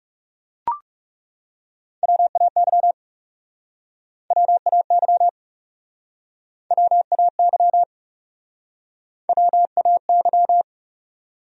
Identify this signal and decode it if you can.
Morse code, heard as a keyed tone, about 37 wpm, 700 Hz. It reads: WAY WAY WAY WAY
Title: Speed-Racing Top 200 Words 30wpm